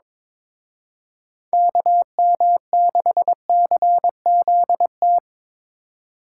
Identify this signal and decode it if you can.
Morse code, heard as a keyed tone, about 26 wpm, 700 Hz. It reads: KM6CZT
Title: Speed Racing - USA 2x3 Call Signs 22wpm